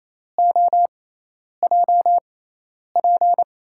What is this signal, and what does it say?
Morse code, heard as a keyed tone, about 28 wpm, spaced as for 10 wpm, 700 Hz. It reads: OJP
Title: USA Call Signs - 5 Characters - 6x Character Spacing - 28wpm